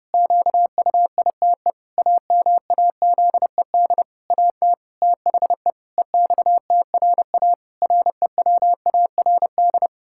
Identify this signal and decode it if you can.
Morse code, heard as a keyed tone, about 30 wpm, 700 Hz. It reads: QUITE AMAZED AT THE EXTRA REWARD
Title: Pangram Challenge v2 - 30wpm